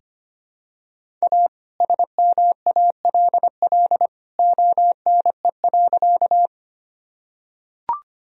Morse code, heard as a keyed tone, about 25 wpm, 700 Hz. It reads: A SMALL ONE.
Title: Sentences from Top 1000 Words - Review Part 1 - 25wpm